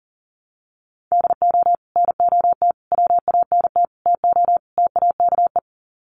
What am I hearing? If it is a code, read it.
Morse code, heard as a keyed tone, about 40 wpm, 700 Hz. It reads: DO NOT WANT TO TAKE
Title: Sets of 5 Words Encore2 40wpm